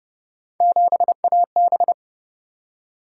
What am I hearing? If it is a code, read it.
Morse code, heard as a keyed tone, about 29 wpm, 700 Hz. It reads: 7AB